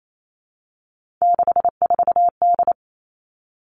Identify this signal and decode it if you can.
Morse code, heard as a keyed tone, about 28 wpm, 700 Hz. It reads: 64D
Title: Three Challenging Letter-Number Combinations 28wpm